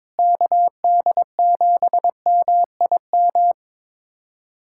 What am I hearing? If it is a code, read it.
Morse code, heard as a keyed tone, about 22 wpm, 700 Hz. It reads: KD7MIM
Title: USA Callsign DE Callsign - 6 Characters - 22wpm